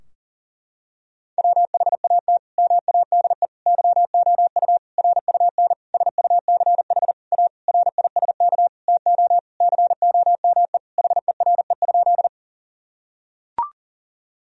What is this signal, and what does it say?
Morse code, heard as a keyed tone, about 40 wpm, 700 Hz. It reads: WHAT MADE YOU RUN SUCH A RISK TO COME HERE?
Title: Sentences from Top 1000 Words - Review Part 4 - 40wpm